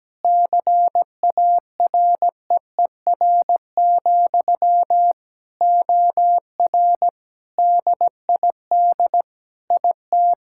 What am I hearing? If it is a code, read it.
Morse code, heard as a keyed tone, about 17 wpm, 700 Hz. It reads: CAREER, OR DID IT
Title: Sentences from Top 1000 Words - Review Part 6 - 17wpm